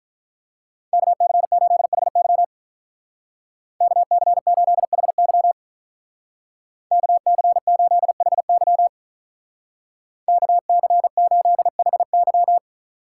Morse code, heard as a keyed tone, about 44 wpm, 700 Hz. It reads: KC8HY KC8HY KC8HY KC8HY
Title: Speed Racing - USA 2x2 Call Signs 35wpm